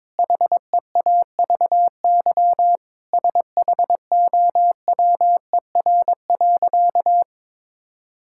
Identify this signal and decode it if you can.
Morse code, heard as a keyed tone, about 22 wpm, 700 Hz. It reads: HEAVY SHOWER.
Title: Sentences from Top 1600 Words 22wpm